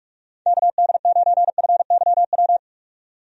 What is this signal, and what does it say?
Morse code, heard as a keyed tone, about 46 wpm, 700 Hz. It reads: KD9FYW
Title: USA Call Signs - 6 Characters - 45wpm